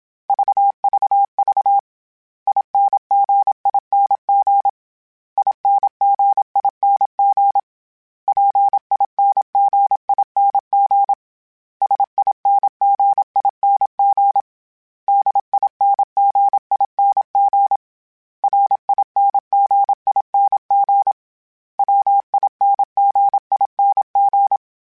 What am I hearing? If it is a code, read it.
Morse code, heard as a keyed tone, about 27 wpm, 800 Hz. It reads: VVV INGING INGING PINGING SINGING DINGING RINGING WINGING
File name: Special INGs / Special INGs-mp3s / Dble ings @25